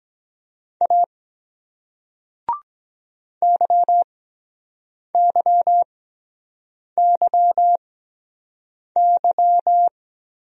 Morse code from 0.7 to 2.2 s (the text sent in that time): A